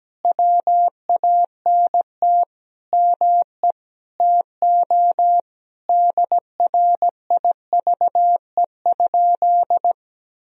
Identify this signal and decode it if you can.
Morse code, heard as a keyed tone, about 17 wpm, 700 Hz. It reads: WANT ME TO DRIVE?
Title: Sentences from Top 1000 Words - Review Part 2 - 17wpm